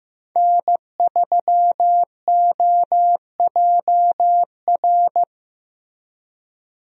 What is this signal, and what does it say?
Morse code, heard as a keyed tone, about 16 wpm, 700 Hz. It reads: N3OJR
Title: USA Call Signs - 5 Characters - Repeated 2x - 15wpm